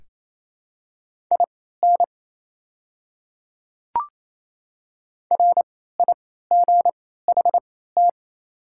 Morse code, heard as a keyed tone, about 28 wpm, spaced as for 16 wpm, 700 Hz. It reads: IN RIGHT
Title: 3x Character Spacing - Top 200 Words - 28wpm